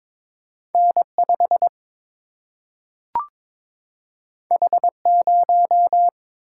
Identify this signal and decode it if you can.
Morse code, heard as a keyed tone, about 23 wpm, 700 Hz. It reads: N 5 H 0
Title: Single-Letter Single-Number 22wpm